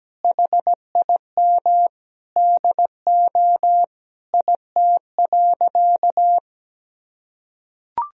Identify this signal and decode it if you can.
Morse code, heard as a keyed tone, about 17 wpm, 700 Hz. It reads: HIM DO IT.
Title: Sentences from Top 1000 Words - Review Part 3 - 17wpm